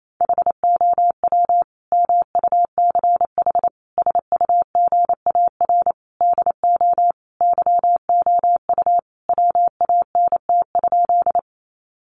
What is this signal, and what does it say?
Morse code, heard as a keyed tone, about 28 wpm, 700 Hz. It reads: HOW MUCH SUGAR DO YOU WANT?
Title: Sentences from Top 1200 Words 28wpm